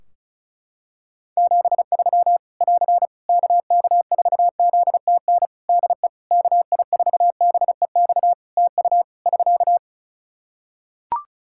Morse code, heard as a keyed tone, about 36 wpm, 700 Hz. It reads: 73 <AR> KK4ZTN DE KI4BEX TU <SK>
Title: K1USN SST QSO Practice 35wpm